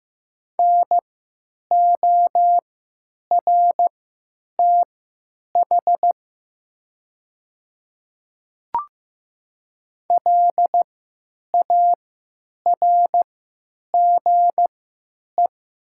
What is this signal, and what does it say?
Morse code, heard as a keyed tone, about 15 wpm, spaced as for 9 wpm, 700 Hz. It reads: NORTH LARGE
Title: 3x Character Spacing - Top 300 Words - 15wpm